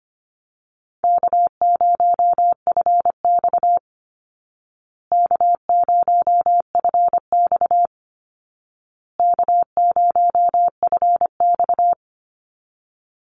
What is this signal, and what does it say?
Morse code, heard as a keyed tone, about 25 wpm, 700 Hz. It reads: K0FX K0FX K0FX
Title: Easy Call Signs Repeated 3x - 25wpm